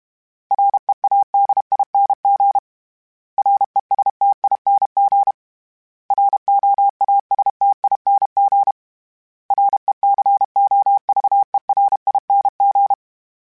Morse code, heard as a keyed tone, about 32 wpm, 800 Hz. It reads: READING RESTING ROASTING RECOVERING
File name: Special INGs-mp3s / Special INGs-mp3s / Sngl Ing @30